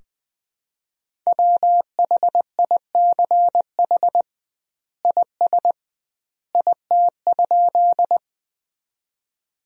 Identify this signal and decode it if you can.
Morse code, heard as a keyed tone, about 20 wpm, 700 Hz. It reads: WHICH IS IT?
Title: 2x Word-Spacing - Sentences from Top 100 Words - 20wpm